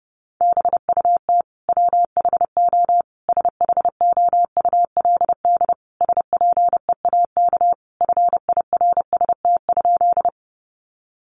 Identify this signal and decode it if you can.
Morse code, heard as a keyed tone, about 30 wpm, 700 Hz. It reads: BUT WHO SHOULD SPEAK FIRST?
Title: Sentences from Top 1100 Words 30wpm